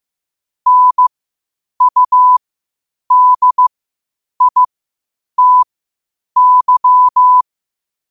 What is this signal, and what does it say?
Morse code, heard as a keyed tone, about 15 wpm, spaced as for 8 wpm, 1000 Hz. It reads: NUDITY
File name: qotdfiles / 8wpm